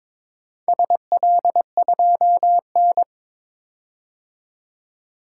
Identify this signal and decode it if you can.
Morse code, heard as a keyed tone, about 22 wpm, 700 Hz. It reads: SL2N
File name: practice / 22wpm / Four Letter-Number Combinations - 22wpm